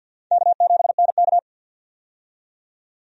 Morse code, heard as a keyed tone, about 48 wpm, 700 Hz. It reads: K7NK
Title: USA Call Signs - 4 Characters - 50wpm